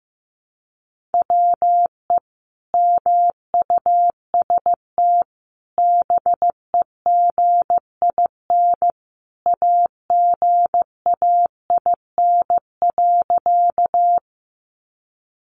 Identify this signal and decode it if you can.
Morse code, heard as a keyed tone, about 15 wpm, 700 Hz. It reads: WE MUST BEGIN AGAIN.
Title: Sentences from Top 300 Words 15wpm